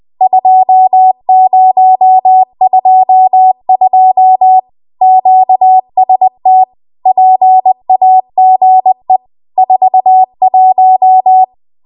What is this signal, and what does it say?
Morse code, heard as a keyed tone, about 20 wpm, 750 Hz. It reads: 2022 QST PAGE 41